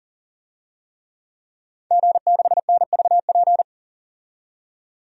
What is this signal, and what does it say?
Morse code, heard as a keyed tone, about 40 wpm, 700 Hz. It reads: G6NVP
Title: DXCC-100 Call Signs - 5 Characters - 40wpm